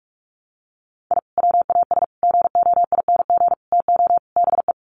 Morse code, heard as a keyed tone, about 44 wpm, 700 Hz. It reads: I WAS GOING TO BE
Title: Sets of 5 Words 45wpm